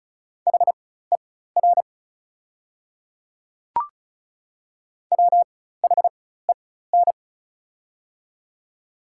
Morse code, heard as a keyed tone, about 36 wpm, spaced as for 16 wpm, 700 Hz. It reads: HER WHEN
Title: 4x Character Spacing - Top 200 Words - 35wpm